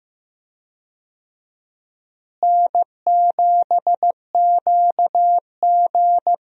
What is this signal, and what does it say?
Morse code, heard as a keyed tone, about 15 wpm, 700 Hz. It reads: N7QG
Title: USA Call Signs - 4 Characters - Repeated 2x - 15wpm